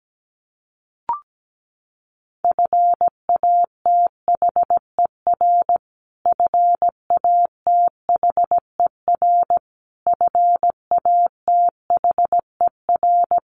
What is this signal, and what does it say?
Morse code, heard as a keyed tone, about 17 wpm, 700 Hz. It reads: FATHER FATHER FATHER
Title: Top 200 Words Repeated 17wpm